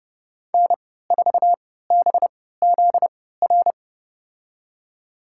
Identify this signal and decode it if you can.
Morse code, heard as a keyed tone, about 30 wpm, spaced as for 17 wpm, 700 Hz. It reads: N4BZR
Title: USA Call Signs - 5 Characters - 3x Character Spacing - 30wpm